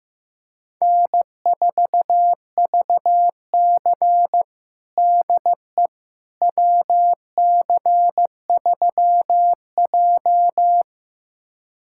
Morse code, heard as a keyed tone, about 15 wpm, 700 Hz. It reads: N4VC DE WC3J
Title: USA Callsign DE Callsign - 4 Characters - 15wpm